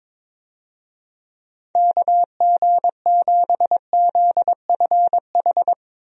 Morse code, heard as a keyed tone, about 22 wpm, 700 Hz. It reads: KG7ZFH
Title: USA Call Signs - 6 Characters - Repeated 2x - 22wpm